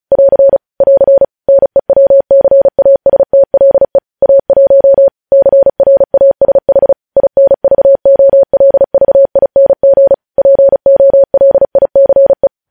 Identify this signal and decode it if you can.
Morse code, heard as a keyed tone, about 36 wpm, 550 Hz. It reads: <AR> <AR> NEWCASTLE A1 CRASH INVOLVING POLICE
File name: podcast / News headlines-1x-35x35